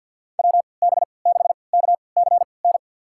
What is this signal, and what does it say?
Morse code, heard as a keyed tone, about 53 wpm, spaced as for 29 wpm, 700 Hz. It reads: WB6KCN
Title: USA Call Signs - 6 Characters - 3x Character Spacing - 50wpm